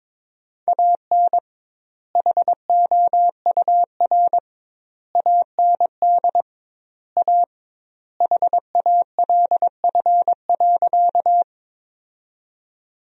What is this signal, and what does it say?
Morse code, heard as a keyed tone, about 22 wpm, 700 Hz. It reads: AN HOUR AND A HALF.
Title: 2x Word-Spacing - Sentences from Top 400 Words - 22wpm